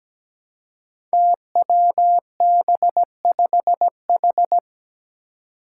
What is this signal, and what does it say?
Morse code, heard as a keyed tone, about 17 wpm, 700 Hz. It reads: TWB5H
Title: Five Challenging Letter-Number Combinations 17wpm